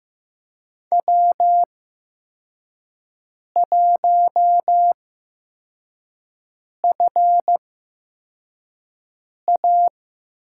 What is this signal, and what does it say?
Morse code, heard as a keyed tone, about 15 wpm, spaced as for 4 wpm, 700 Hz. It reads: W1FA
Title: USA Call Signs - 8x Character Spacing - 15wpm